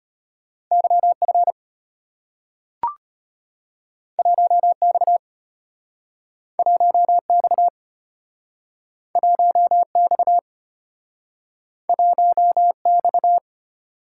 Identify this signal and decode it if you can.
Morse code, heard as a keyed tone, about 34 wpm, 700 Hz. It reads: YF 1X 1X 1X 1X